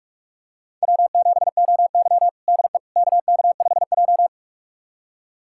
Morse code, heard as a keyed tone, about 47 wpm, 700 Hz. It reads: W7OY DE KK5J
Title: USA Callsign DE Callsign - 4 Characters - 45wpm